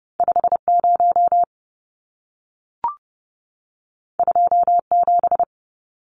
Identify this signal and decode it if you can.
Morse code, heard as a keyed tone, about 30 wpm, 700 Hz. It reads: 50 27
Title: Numbers - 2-Digit 30wpm